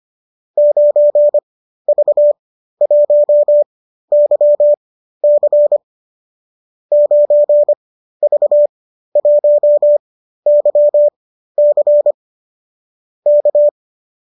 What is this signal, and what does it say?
Morse code, heard as a keyed tone, about 25 wpm, spaced as for 13 wpm, 590 Hz. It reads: 9V1YC 9V1YC K
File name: qso207 13